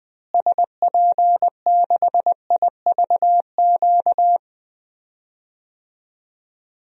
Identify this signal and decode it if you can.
Morse code, heard as a keyed tone, about 20 wpm, 700 Hz. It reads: SP6IVQ
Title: DXCC-100 Call Signs - 6 Characters - Repeated 2x - 20wpm